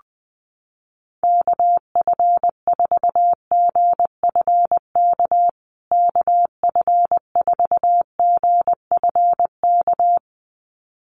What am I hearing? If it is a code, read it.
Morse code, heard as a keyed tone, about 20 wpm, 700 Hz. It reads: KF4GFK KF4GFK
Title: USA Call Signs Repeated 2x 20wpm